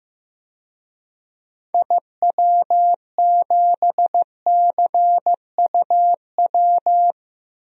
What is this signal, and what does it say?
Morse code, heard as a keyed tone, about 15 wpm, 700 Hz. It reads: IW7CUW
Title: DXCC-100 Call Signs - 6 Characters - 15wpm